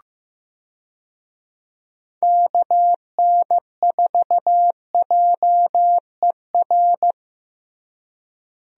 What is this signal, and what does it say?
Morse code, heard as a keyed tone, about 15 wpm, 700 Hz. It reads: KN4JER
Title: USA Call Signs - 6 Characters - Repeated 2x - 15wpm